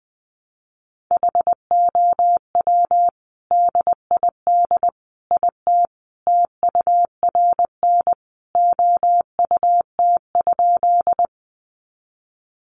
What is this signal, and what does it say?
Morse code, heard as a keyed tone, about 19 wpm, 700 Hz. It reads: HOW DID IT TURN OUT?
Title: A Sound Of Thunder 20wpm